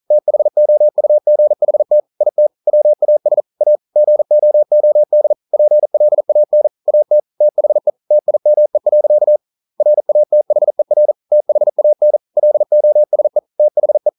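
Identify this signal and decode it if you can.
Morse code, heard as a keyed tone, about 40 wpm, 600 Hz. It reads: THOUGHT IT WAS A GOOD PLAN AT THE TIME. RATHER THAN LOSE THE
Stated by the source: Morse code practice at 41 wpm, from CW med Gnister.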